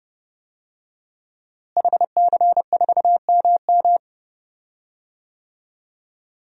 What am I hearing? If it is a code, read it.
Morse code, heard as a keyed tone, about 30 wpm, 700 Hz. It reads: HC4MM
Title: DXCC-100 Call Signs - 5 Characters - Repeated 2x - 30wpm